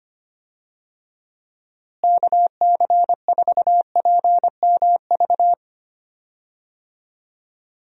Morse code, heard as a keyed tone, about 25 wpm, 700 Hz. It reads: KC4PMV
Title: USA Call Signs - 6 Characters - Repeated 3x - 25wpm